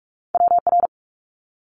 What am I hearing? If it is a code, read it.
Morse code, heard as a keyed tone, about 46 wpm, 700 Hz. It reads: WR
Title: Multi-letter Phonemes Quick Fire 45wpm